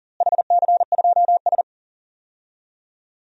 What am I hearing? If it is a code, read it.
Morse code, heard as a keyed tone, about 37 wpm, 700 Hz. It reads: HC2S
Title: DXCC-100 Call Signs - 4 Characters - Repeated 3x - 40wpm